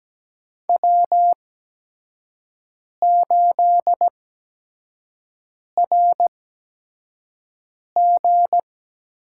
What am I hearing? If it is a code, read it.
Morse code, heard as a keyed tone, about 17 wpm, spaced as for 5 wpm, 700 Hz. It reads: W8RG
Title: USA Call Signs - 4 Characters - 8x Character Spacing - 17wpm